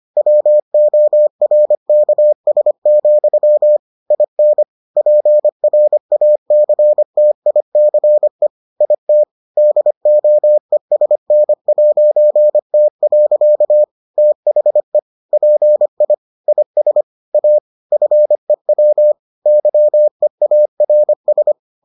Morse code, heard as a keyed tone, about 25 wpm, 600 Hz. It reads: WORKS, IN PRACTICE IT DOESN'T. THE PI IS A FEW YEARS